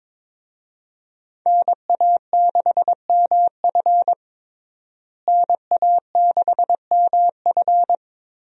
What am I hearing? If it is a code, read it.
Morse code, heard as a keyed tone, about 22 wpm, 700 Hz. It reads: NA6MF NA6MF
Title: USA Call Signs - 5 Characters - Repeated 2x - 22wpm